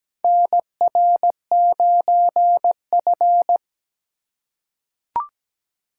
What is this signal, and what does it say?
Morse code, heard as a keyed tone, about 17 wpm, 700 Hz. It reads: NR9F
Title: USA Call Signs - 4 Characters - 3x Character Spacing - 17wpm